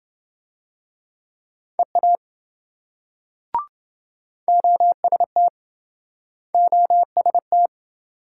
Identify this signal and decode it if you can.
Morse code, heard as a keyed tone, about 27 wpm, 700 Hz. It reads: EA OST OST